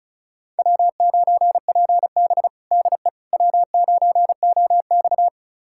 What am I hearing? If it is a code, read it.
Morse code, heard as a keyed tone, about 36 wpm, 700 Hz. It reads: W9PB DE W9OX
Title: USA Callsign DE Callsign - 4 Characters - 35wpm